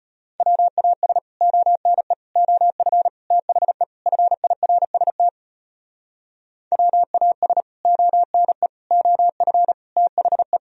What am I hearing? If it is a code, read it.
Morse code, heard as a keyed tone, about 33 wpm, 700 Hz. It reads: WAS ONE OF THE FIRST WAS ONE OF THE